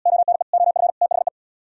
Morse code, heard as a keyed tone, about 73 wpm, 700 Hz. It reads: ONE OF THE